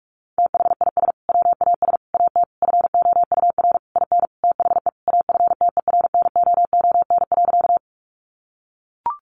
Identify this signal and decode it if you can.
Morse code, heard as a keyed tone, about 44 wpm, 700 Hz. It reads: THIS WAS AT FOUR IN THE AFTERNOON.
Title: Sentences from Top 1500 Words 45wpm